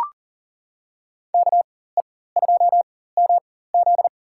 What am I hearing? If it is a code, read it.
Morse code, heard as a keyed tone, about 40 wpm, spaced as for 19 wpm, 700 Hz. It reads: KE2MZ
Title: USA Call Signs - 4x Character Spacing - 40wpm